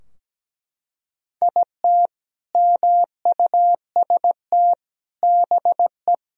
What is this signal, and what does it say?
Morse code, heard as a keyed tone, about 17 wpm, 700 Hz. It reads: IT MUST BE
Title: Sentences from Top 1000 Words - Review Part 2 - 17wpm